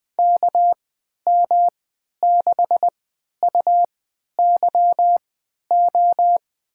Morse code, heard as a keyed tone, about 21 wpm, spaced as for 11 wpm, 700 Hz. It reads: KM6UYO
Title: USA Call Signs - 6 Characters - 3x Character Spacing - 20wpm